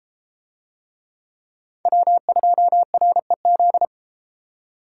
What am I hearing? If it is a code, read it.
Morse code, heard as a keyed tone, about 33 wpm, 700 Hz. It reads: W2REZ